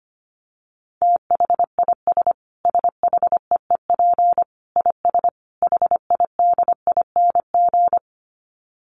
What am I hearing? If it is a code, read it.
Morse code, heard as a keyed tone, about 25 wpm, 700 Hz. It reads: THIS SHEEP IS HIDING
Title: Dit Challenge 25wpm